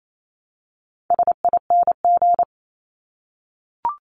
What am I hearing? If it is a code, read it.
Morse code, heard as a keyed tone, about 28 wpm, 700 Hz. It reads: SING
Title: Top 500 Words Repeated 28wpm